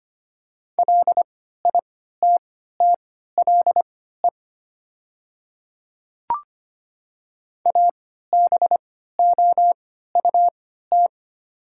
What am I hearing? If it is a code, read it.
Morse code, heard as a keyed tone, about 25 wpm, spaced as for 14 wpm, 700 Hz. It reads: LITTLE ABOUT
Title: 3x Character Spacing - Top 300 Words - 25wpm